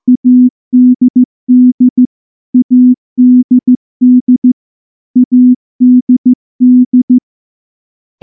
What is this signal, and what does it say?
Morse code, heard as a keyed tone, about 14 wpm, 250 Hz. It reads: ADD ADD ADD
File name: morse-training-15WPM